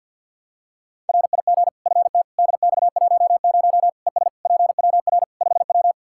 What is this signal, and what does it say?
Morse code, heard as a keyed tone, about 52 wpm, 700 Hz. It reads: RIG FT DX10 ES PWR 5W
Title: Practice QSO - Exchange 2 - Info - 50wpm